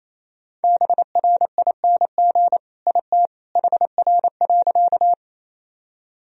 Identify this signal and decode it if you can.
Morse code, heard as a keyed tone, about 28 wpm, 700 Hz. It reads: BRING IT HR.
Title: Sentences from Top 500 Words and Abbreviations 28wpm